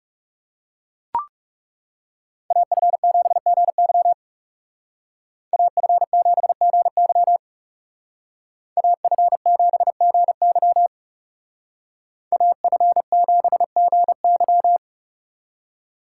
Morse code, heard as a keyed tone, about 35 wpm, 700 Hz. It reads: AF7GY AF7GY AF7GY AF7GY
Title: Speed Racing - USA 2x2 Call Signs 30wpm